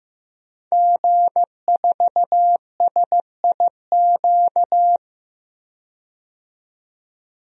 Morse code, heard as a keyed tone, about 15 wpm, 700 Hz. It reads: G4SIQ
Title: DXCC-100 Call Signs - 5 Characters - 15wpm